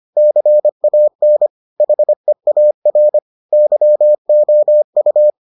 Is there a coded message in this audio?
Yes: Morse code from 0.2 to 5.3 s, heard as a keyed tone, about 25 wpm, 600 Hz. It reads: CAN HEAR YOU